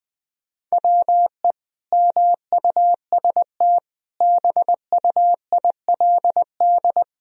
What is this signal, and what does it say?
Morse code, heard as a keyed tone, about 21 wpm, 700 Hz. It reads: WE MUST BUILD
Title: Sentences from Top 1000 Words - Review Part 1 - 20wpm